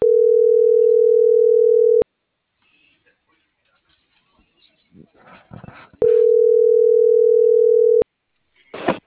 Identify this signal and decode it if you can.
Ringback tone (North America), 440 + 480 Hz.